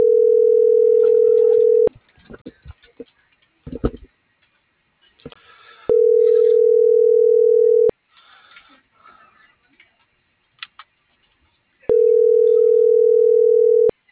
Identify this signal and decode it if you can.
Ringback tone (North America), 440 + 480 Hz.